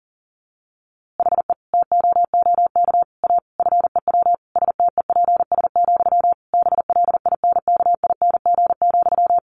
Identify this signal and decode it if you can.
Morse code, heard as a keyed tone, about 40 wpm, 700 Hz. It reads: HE TOOK A FEW STEPS, BLINKING,
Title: A Sound Of Thunder 40wpm